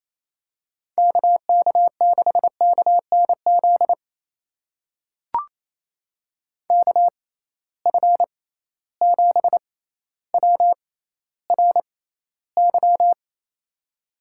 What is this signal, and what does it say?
Morse code, heard as a keyed tone, about 28 wpm, spaced as for 14 wpm, 700 Hz. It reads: KK6KNZ KF7WRY
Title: USA Call Signs - 6 Characters - 6x Character Spacing - 28wpm